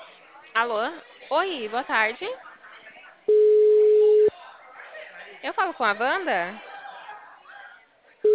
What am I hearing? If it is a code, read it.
Ringback tone (Europe, 425 Hz), with speech.